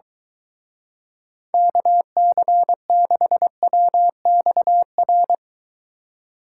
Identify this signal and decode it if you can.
Morse code, heard as a keyed tone, about 23 wpm, 700 Hz. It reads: KC6WXR